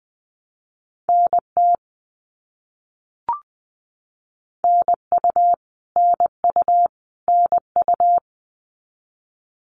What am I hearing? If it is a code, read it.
Morse code, heard as a keyed tone, about 20 wpm, 700 Hz. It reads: NT NU NU NU
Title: Canadian Provinces and Territories - Repeated 3x - 20wpm